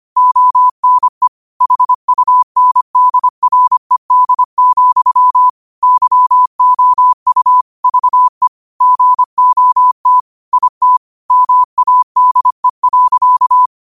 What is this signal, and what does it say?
Morse code, heard as a keyed tone, about 25 wpm, 1000 Hz. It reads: ONE HUNDRED, YOU VE GOT IT MADE.
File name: qotdfiles / 25wpm